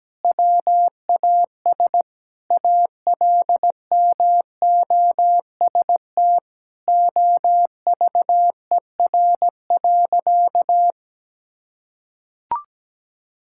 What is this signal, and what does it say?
Morse code, heard as a keyed tone, about 17 wpm, 700 Hz. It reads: WAS ALMOST OVER.
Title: Sentences from Top 1600 Words 17wpm